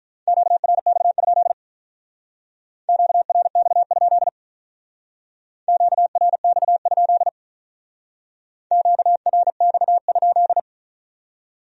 Morse code, heard as a keyed tone, about 43 wpm, 700 Hz. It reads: QRX? QRX? QRX? QRX?
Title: Speed-Racing Q-Codes - 35wpm